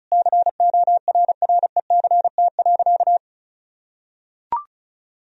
Morse code, heard as a keyed tone, about 36 wpm, 700 Hz. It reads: CORRECT.
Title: Sentences from Top 1000 Words - Review Part 4 - 35wpm